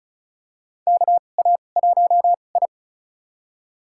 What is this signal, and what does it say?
Morse code, heard as a keyed tone, about 34 wpm, spaced as for 25 wpm, 700 Hz. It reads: KA1I
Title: USA Call Signs - 2x Character Spacing - 35wpm